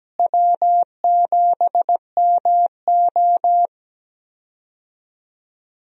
Morse code, heard as a keyed tone, about 17 wpm, 700 Hz. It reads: W7MO
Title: USA Call Signs - 4 Characters - Repeated 3x - 17wpm